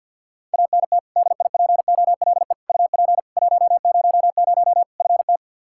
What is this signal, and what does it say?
Morse code, heard as a keyed tone, about 51 wpm, 700 Hz. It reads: ANT DIPOLE UP 100 FT
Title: Practice QSO - Exchange 2 - Info - 50wpm